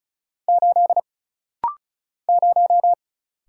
Morse code, heard as a keyed tone, about 35 wpm, 700 Hz. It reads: 8 0
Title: Single Letter-Number - Rapid-Fire 35wpm